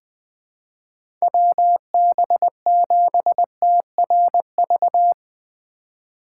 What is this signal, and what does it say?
Morse code, heard as a keyed tone, about 20 wpm, 700 Hz. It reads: WB7TRV
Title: USA Call Signs - 6 Characters - 2x Character Spacing - 20wpm